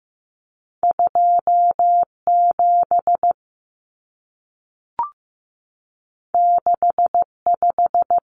Morse code, heard as a keyed tone, about 15 wpm, 700 Hz. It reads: 27 65
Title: Numbers - 2-Digit 15wpm